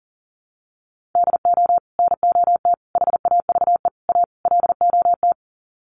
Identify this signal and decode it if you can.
Morse code, heard as a keyed tone, about 40 wpm, 700 Hz. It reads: DO NOT HAVE A LOT
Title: Sets of 5 Words Encore 40wpm